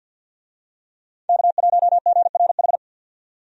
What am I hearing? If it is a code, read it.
Morse code, heard as a keyed tone, about 52 wpm, 700 Hz. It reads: K1GRH
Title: USA Call Signs - 5 Characters - 50wpm